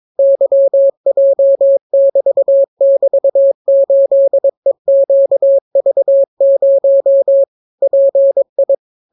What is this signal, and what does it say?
Morse code, heard as a keyed tone, about 22 wpm, 550 Hz. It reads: YJ<BT><BT>8EQV0 PI